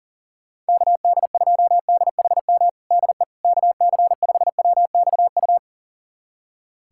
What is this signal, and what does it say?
Morse code, heard as a keyed tone, about 37 wpm, 700 Hz. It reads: KD2DHM DE KC5WXU
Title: USA Callsign DE Callsign - 6 Characters - 40wpm